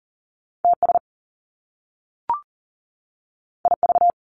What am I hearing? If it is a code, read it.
Morse code, heard as a keyed tone, about 38 wpm, 700 Hz. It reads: TS IV
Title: Top Two Letter Combinations in Words 40wpm